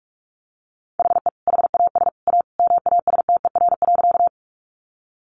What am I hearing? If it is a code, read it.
Morse code, heard as a keyed tone, about 44 wpm, 700 Hz. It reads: HE HAS A MASTER.
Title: Sentences from Top 1300 Words 45wpm